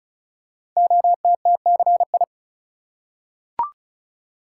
Morse code, heard as a keyed tone, about 36 wpm, 700 Hz. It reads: OTTCI
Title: Code Groups - 5 Letters - 35wpm